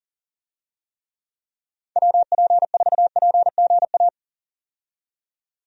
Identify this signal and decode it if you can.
Morse code, heard as a keyed tone, about 40 wpm, 700 Hz. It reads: WP4PGA
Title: USA Call Signs - 6 Characters - 40wpm